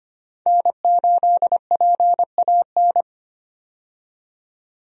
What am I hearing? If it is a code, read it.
Morse code, heard as a keyed tone, about 25 wpm, 690 Hz. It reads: N8PAN
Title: USA Call Signs - 5 Characters - 25wpm